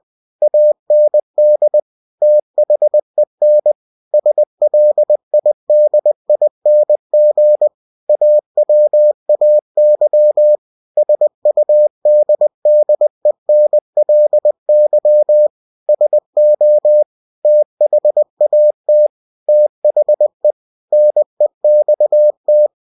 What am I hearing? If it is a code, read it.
Morse code, heard as a keyed tone, about 20 wpm, 600 Hz. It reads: AND THEN SLIDING AWAY SUDDENLY SO THAT THE NEXT
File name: cw-books / winnie-the-pooh / winnie-20wpm-page-0028